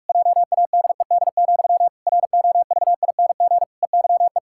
Morse code, heard as a keyed tone, about 46 wpm, 700 Hz. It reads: JADED, ROVING EYE
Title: Pangram Challenge v2 - 45wpm